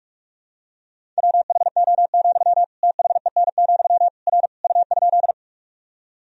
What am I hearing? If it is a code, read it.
Morse code, heard as a keyed tone, about 46 wpm, 700 Hz. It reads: WHO, THEN, R U?
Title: Sentences from Top 100 Words and Abbreviations 45wpm